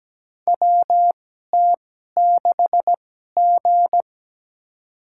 Morse code, heard as a keyed tone, about 17 wpm, spaced as for 12 wpm, 700 Hz. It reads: WT6G
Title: USA Call Signs - 2x Character Spacing - 17wpm